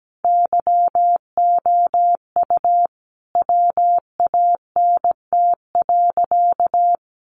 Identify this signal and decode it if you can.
Morse code, heard as a keyed tone, about 17 wpm, 700 Hz. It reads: YOU WANT.
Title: Sentences from Top 1100 Words 17wpm